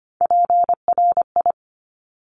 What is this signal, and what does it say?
Morse code, heard as a keyed tone, about 25 wpm, 700 Hz. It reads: PRI
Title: Top Three Letter Combinations in Words 25wpm